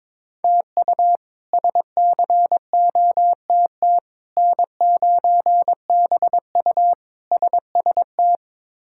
Morse code, heard as a keyed tone, about 21 wpm, 700 Hz. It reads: TU SCOTT N9BU SST
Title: K1USN SST QSO Practice - 22wpm